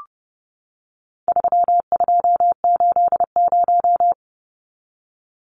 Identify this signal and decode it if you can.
Morse code, heard as a keyed tone, about 30 wpm, 700 Hz. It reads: 3280
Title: Numbers - 4-Digit 30wpm